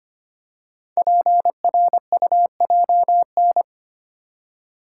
Morse code, heard as a keyed tone, about 25 wpm, 700 Hz. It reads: PRUJN
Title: Code Groups - 5 Letters - 25wpm